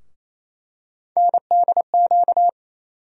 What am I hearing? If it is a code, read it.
Morse code, heard as a keyed tone, about 28 wpm, 700 Hz. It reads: NDQ
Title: USA Call Sign Postfixes - 3 Letters 28wpm